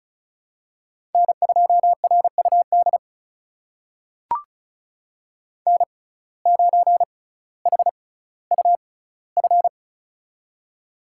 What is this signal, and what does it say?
Morse code, heard as a keyed tone, about 36 wpm, spaced as for 18 wpm, 700 Hz. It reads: N2RUD N9HUF
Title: USA Call Signs - 5 Characters - 6x Character Spacing - 35wpm